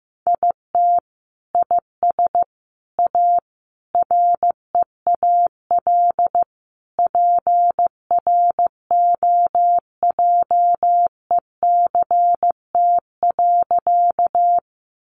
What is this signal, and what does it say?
Morse code, heard as a keyed tone, about 15 wpm, 700 Hz. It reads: IT IS A REAL PROJECT.
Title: Sentences from Top 1400 Words 15wpm